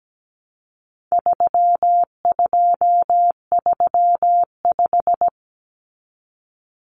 Morse code, heard as a keyed tone, about 17 wpm, 700 Hz. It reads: 3235
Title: Numbers - 4-Digit 17wpm